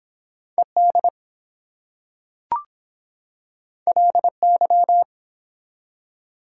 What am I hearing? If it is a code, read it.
Morse code, heard as a keyed tone, about 23 wpm, 700 Hz. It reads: ED LY